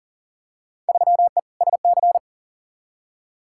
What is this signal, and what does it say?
Morse code, heard as a keyed tone, about 40 wpm, 700 Hz. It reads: 3E SC
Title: Field Day Exchanges Encore - 40wpm